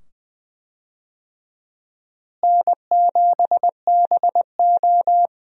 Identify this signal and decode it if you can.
Morse code, heard as a keyed tone, about 20 wpm, 700 Hz. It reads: N7BO
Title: USA Call Signs - 4 Characters - Repeated 2x - 20wpm